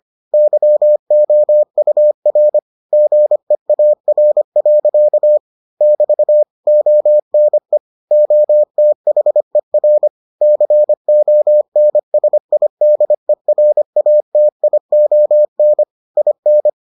Morse code, heard as a keyed tone, about 25 wpm, 600 Hz. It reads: YOUR GEAR. <BT> ONE OTHER CONSIDERATION IN